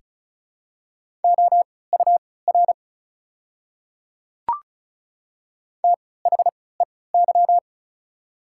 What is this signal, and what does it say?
Morse code, heard as a keyed tone, about 36 wpm, spaced as for 20 wpm, 690 Hz. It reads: OUR THEY
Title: 3x Character Spacing - Top 200 Words - 35wpm